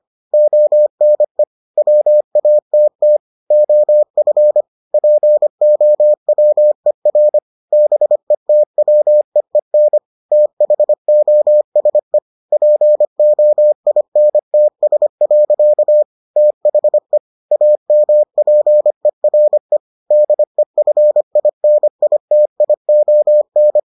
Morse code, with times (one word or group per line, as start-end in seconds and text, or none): 0.3-1.4 ONE
1.8-3.2 WATT
3.5-4.6 OF
4.9-7.4 POWER
7.7-10.0 BETWEEN
10.3-12.2 THOSE
12.5-16.0 POINTS.
16.4-17.2 THE
17.5-19.8 AMPERE
20.1-23.8 DEFINITION